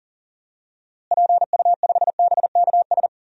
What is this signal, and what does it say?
Morse code, heard as a keyed tone, about 40 wpm, 700 Hz. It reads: PU5BKS
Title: DXCC-100 Call Signs - 6 Characters - Repeated 3x - 40wpm